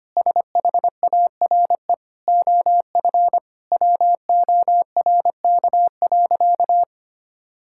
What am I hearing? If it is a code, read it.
Morse code, heard as a keyed tone, about 25 wpm, 700 Hz. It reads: SHARE OF WORK.
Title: Sentences from Top 1000 Words - Review Part 6 - 25wpm